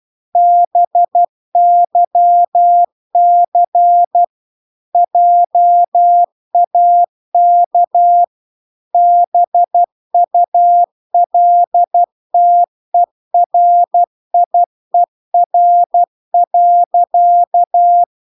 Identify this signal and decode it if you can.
Morse code, heard as a keyed tone, about 12 wpm, 700 Hz. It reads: BYC JAK BULTERIER.